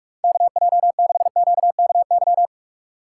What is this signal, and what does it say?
Morse code, heard as a keyed tone, about 46 wpm, 700 Hz. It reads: KJ6QKY
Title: USA Call Signs - 6 Characters - 2x Character Spacing - 45wpm